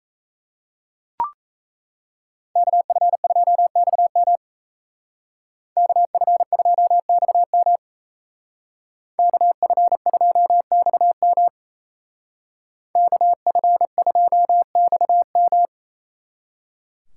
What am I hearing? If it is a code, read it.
Morse code, heard as a keyed tone, about 33 wpm, 700 Hz. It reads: KF2XM KF2XM KF2XM KF2XM
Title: Speed Racing - USA 2x2 Call Signs 28wpm